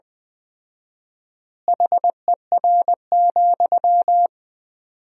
Morse code, heard as a keyed tone, about 21 wpm, 700 Hz. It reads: HER,
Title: 4x Word-Spacing - Sentences from Top 200 Words - 20wpm